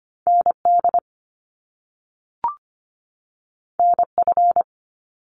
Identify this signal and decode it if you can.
Morse code, heard as a keyed tone, about 25 wpm, 700 Hz. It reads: ND NF
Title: Top Two Letter Combinations in Words 25wpm